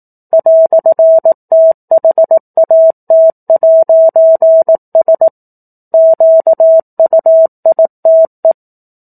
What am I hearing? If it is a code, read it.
Morse code, heard as a keyed tone, about 18 wpm, 650 Hz. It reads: "THAT'S QUITE
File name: 04-Chapter-04 18 WPM